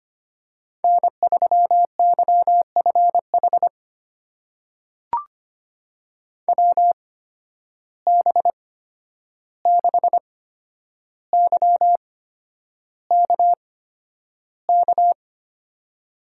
Morse code, heard as a keyed tone, about 25 wpm, spaced as for 10 wpm, 700 Hz. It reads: N3YFH WB6YKK
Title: USA Call Signs - 8x Character Spacing - 25wpm